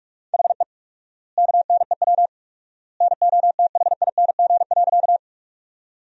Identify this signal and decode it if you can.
Morse code, heard as a keyed tone, about 43 wpm, 700 Hz. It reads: HE KNEW NOTHING.
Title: 4x Word-Spacing - Sentences from Top 500 Words - 45wpm